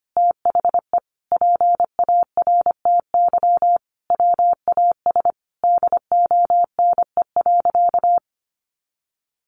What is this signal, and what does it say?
Morse code, heard as a keyed tone, about 25 wpm, 700 Hz. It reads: THE PARTY WAS DONE.
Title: Sentences from Top 1400 Words 25wpm